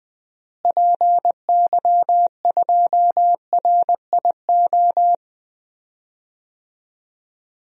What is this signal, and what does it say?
Morse code, heard as a keyed tone, about 20 wpm, 700 Hz. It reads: PY2RIO
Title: DXCC-100 Call Signs - 6 Characters - Repeated 3x - 20wpm